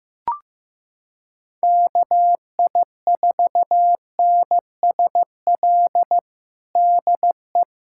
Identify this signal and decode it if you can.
Morse code, heard as a keyed tone, about 15 wpm, 700 Hz. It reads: KI4NSL DE
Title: USA Callsign DE Callsign - 6 Characters - 15wpm